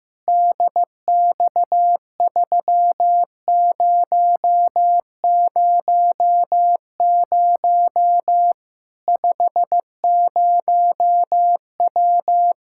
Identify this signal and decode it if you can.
Morse code, heard as a keyed tone, about 15 wpm, 700 Hz. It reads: DX3000 50W
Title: Practice QSO - Exchange 2 - Info - 15wpm